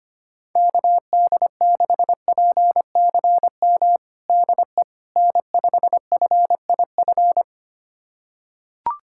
Morse code, heard as a keyed tone, about 25 wpm, 700 Hz. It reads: KD6PCM DE N5FIF
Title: USA Callsign DE Callsign 25wpm